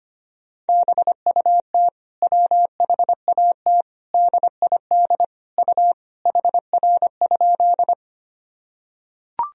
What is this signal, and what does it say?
Morse code, heard as a keyed tone, about 25 wpm, 700 Hz. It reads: BUT WHAT DID U HR?
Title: Sentences from Top 300 Words and Abbreviations 25wpm